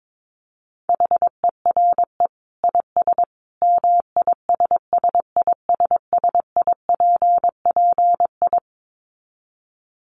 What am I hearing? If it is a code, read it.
Morse code, heard as a keyed tone, about 22 wpm, 700 Hz. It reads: HERE IS MISSISSIPPI
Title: Dit Challenge 22wpm